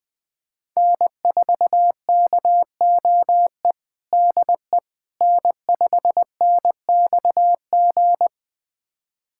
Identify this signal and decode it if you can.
Morse code, heard as a keyed tone, about 20 wpm, 700 Hz. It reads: N4KOE DE N5NXG
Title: USA Callsign DE Callsign - 5 Characters - 20wpm